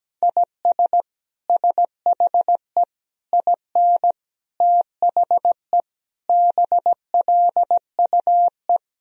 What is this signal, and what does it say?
Morse code, heard as a keyed tone, about 17 wpm, 700 Hz. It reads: IS SHE IN THE BLUE
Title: Sentences from Top 1000 Words - Review Part 3 - 17wpm